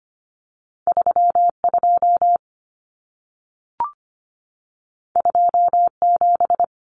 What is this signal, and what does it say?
Morse code, heard as a keyed tone, about 25 wpm, 700 Hz. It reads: 32 27
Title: Common Cut Numbers - 2-Digit 25wpm